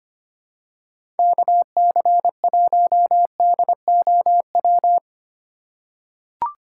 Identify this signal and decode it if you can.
Morse code, heard as a keyed tone, about 25 wpm, 700 Hz. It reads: KC1DOW
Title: USA Call Signs - 2x Character Spacing - 25wpm